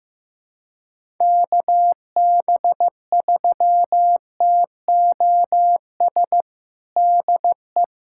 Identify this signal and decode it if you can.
Morse code, heard as a keyed tone, about 15 wpm, 700 Hz. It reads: KB3TOS DE
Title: USA Callsign DE Callsign - 6 Characters - 15wpm